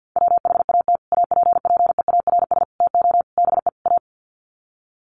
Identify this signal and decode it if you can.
Morse code, heard as a keyed tone, about 48 wpm, 700 Hz. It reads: WHAT APPEARS TO BE A
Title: Sets of 5 Words Encore2 50wpm